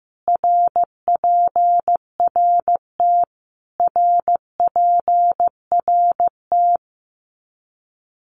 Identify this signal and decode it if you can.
Morse code, heard as a keyed tone, about 15 wpm, 700 Hz. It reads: RPRT RPRT
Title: Abbreviations Repeated 3x - 15wpm